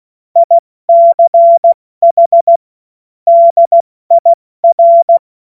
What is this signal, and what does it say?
Morse code, heard as a keyed tone, about 16 wpm, 680 Hz. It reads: ICH DIR
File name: Morsekurs / Witze / Witz 20. Januar 2026 70 F80